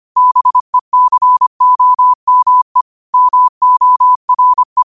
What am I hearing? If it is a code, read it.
Morse code, heard as a keyed tone, about 25 wpm, 1000 Hz. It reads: BECOME MORE
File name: 25wpm